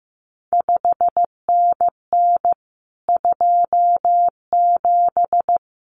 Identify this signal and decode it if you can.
Morse code, heard as a keyed tone, about 15 wpm, 700 Hz. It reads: 5NN 27
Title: CQ WW DX Contest Exchange Encore - 15wpm